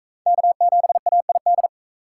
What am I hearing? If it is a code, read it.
Morse code, heard as a keyed tone, about 39 wpm, 700 Hz. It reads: K7AID